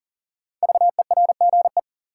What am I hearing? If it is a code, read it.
Morse code, heard as a keyed tone, about 39 wpm, 700 Hz. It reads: VERGE
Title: Sets of 4 Words - New Word Drill - 40wpm